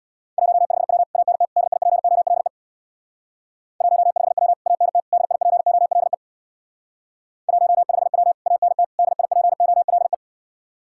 Morse code, heard as a keyed tone, about 64 wpm, 700 Hz. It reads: 15W ANT DIPOLE 15W ANT DIPOLE 15W ANT DIPOLE